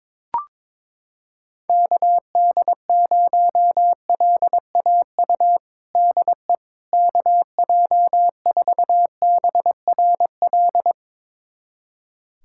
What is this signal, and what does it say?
Morse code, heard as a keyed tone, about 22 wpm, 700 Hz. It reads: KD0LAU DE KJ4BRL
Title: USA Callsign DE Callsign 22wpm